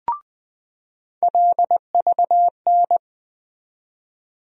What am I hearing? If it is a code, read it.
Morse code, heard as a keyed tone, about 21 wpm, 700 Hz. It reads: LVN
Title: USA Call Sign Postfixes - 3 Letters 20wpm